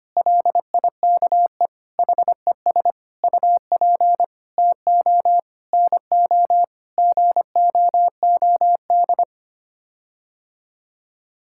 Morse code, heard as a keyed tone, about 25 wpm, 700 Hz. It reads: LIKE HES UP TO NO GOOD
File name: Sets of 9 Words - 25wpm